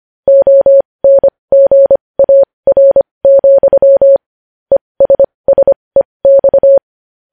Morse code, heard as a keyed tone, about 25 wpm, spaced as for 20 wpm, 550 Hz. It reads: ONGAR, ESSEX
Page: news headlines in morse code at 20 wpm